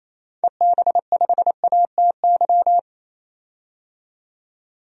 Morse code, heard as a keyed tone, about 28 wpm, 700 Hz. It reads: EB5ATY
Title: DXCC-100 Call Signs - 6 Characters - 28wpm